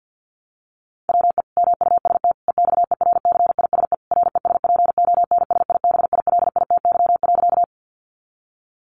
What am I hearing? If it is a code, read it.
Morse code, heard as a keyed tone, about 50 wpm, 700 Hz. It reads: WE MUST EXERCISE RESPONSIBILITY.
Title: Sentences from Top 1200 Words 50wpm